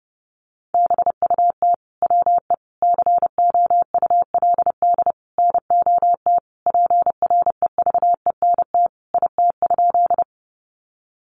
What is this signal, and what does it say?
Morse code, heard as a keyed tone, about 30 wpm, 700 Hz. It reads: BUT WE COULD NOT PREVENT IT?
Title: Sentences from Top 1100 Words 30wpm